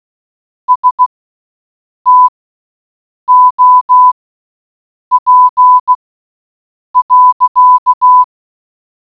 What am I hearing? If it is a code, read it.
Morse code, heard as a keyed tone, about 16 wpm, spaced as for 7 wpm, 1000 Hz. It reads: STOP.